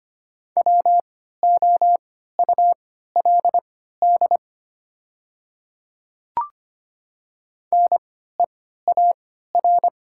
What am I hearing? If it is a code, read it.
Morse code, heard as a keyed tone, about 25 wpm, spaced as for 14 wpm, 690 Hz. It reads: WOULD NEAR
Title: 3x Character Spacing - Top 500 Words - 25wpm